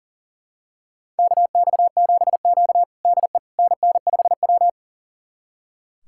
Morse code, heard as a keyed tone, about 40 wpm, 700 Hz. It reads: KX7Q DE NN5W
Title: USA Callsign DE Callsign - 4 Characters - 40wpm